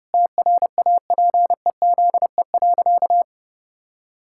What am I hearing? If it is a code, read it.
Morse code, heard as a keyed tone, about 30 wpm, 700 Hz. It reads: TRAPEZE.
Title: Pangram Challenge v2 - 30wpm